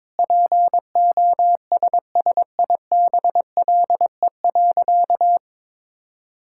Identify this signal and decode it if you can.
Morse code, heard as a keyed tone, about 22 wpm, 700 Hz. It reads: POSSIBLE.
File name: practice / 22wpm / Sentences with 3-Word Phrases - 22wpm